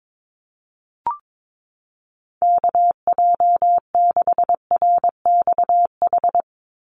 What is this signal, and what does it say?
Morse code, heard as a keyed tone, about 22 wpm, 700 Hz. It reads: KJ6RXH
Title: USA Call Signs 22wpm